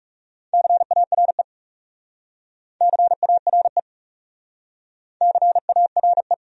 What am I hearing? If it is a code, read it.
Morse code, heard as a keyed tone, about 39 wpm, 700 Hz. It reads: CARE CARE CARE